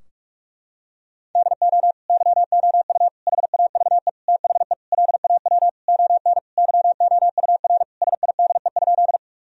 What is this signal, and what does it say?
Morse code, heard as a keyed tone, about 46 wpm, 700 Hz. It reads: DO YOU HAVE THE LAW ON YOUR SIDE?
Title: Sentences from Top 1000 Words - Review Part 3 - 45wpm